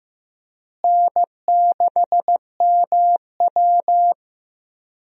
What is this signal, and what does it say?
Morse code, heard as a keyed tone, about 15 wpm, 700 Hz. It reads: N6MW
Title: USA Call Signs - 4 Characters - Repeated 3x - 15wpm